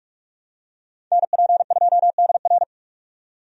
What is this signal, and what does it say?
Morse code, heard as a keyed tone, about 46 wpm, 700 Hz. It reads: NP2DR
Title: USA Call Signs - 5 Characters - 45wpm